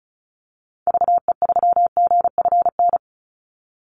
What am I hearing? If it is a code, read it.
Morse code, heard as a keyed tone, about 34 wpm, 700 Hz. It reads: VE3GFN
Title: Call Signs - 35wpm